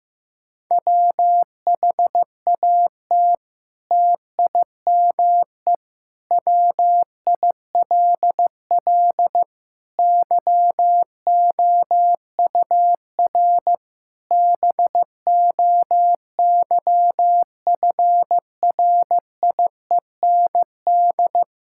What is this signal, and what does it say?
Morse code, heard as a keyed tone, about 16 wpm, 700 Hz. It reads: WHAT TIME WILL YOUR BOYFRIEND
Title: Sentences from Top 1000 Words - Review Part 5 - 15wpm